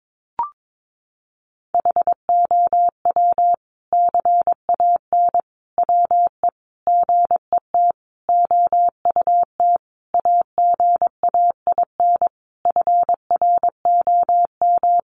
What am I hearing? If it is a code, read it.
Morse code, heard as a keyed tone, about 22 wpm, 700 Hz. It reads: HOW CAN WE GET OUT AGAIN FROM
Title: Sentences from Top 1200 Words 22wpm